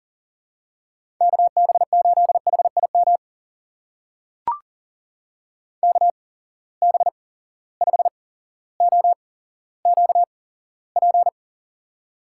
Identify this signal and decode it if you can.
Morse code, heard as a keyed tone, about 40 wpm, spaced as for 17 wpm, 700 Hz. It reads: KB8HIM KB5OQP
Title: USA Call Signs - 6 Characters - 8x Character Spacing - 40wpm